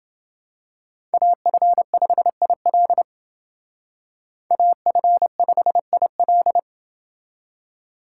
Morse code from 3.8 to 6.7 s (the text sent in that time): AF5IL